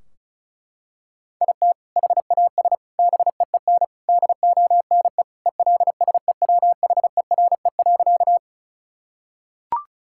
Morse code, heard as a keyed tone, about 34 wpm, 700 Hz. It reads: IT HAS BEEN DONE ELSEWHERE.
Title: Sentences from Top 1600 Words 35wpm